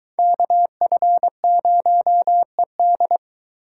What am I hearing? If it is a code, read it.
Morse code, heard as a keyed tone, about 22 wpm, 700 Hz. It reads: KF0ED